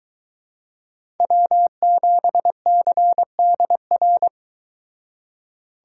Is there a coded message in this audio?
Yes: Morse code from 1.2 to 4.3 s, heard as a keyed tone, about 23 wpm, 700 Hz. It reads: W7CDR